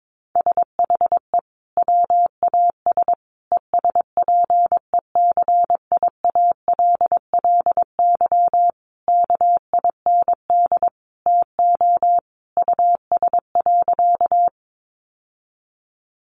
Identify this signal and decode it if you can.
Morse code, heard as a keyed tone, about 22 wpm, 700 Hz. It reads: SHE WAS ESPECIALLY KIND TO US.
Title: Sentences from Top 700 Words 22wpm